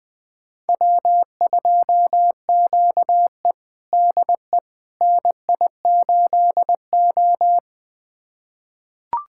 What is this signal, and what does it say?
Morse code, heard as a keyed tone, about 21 wpm, 700 Hz. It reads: W2QE DE NI8O
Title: USA Callsign DE Callsign - 4 Characters - 20wpm